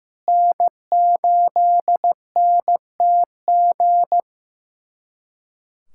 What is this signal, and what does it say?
Morse code, heard as a keyed tone, about 16 wpm, 700 Hz. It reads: N8NTG
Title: USA Callsign DE Callsign - 5 Characters - 15wpm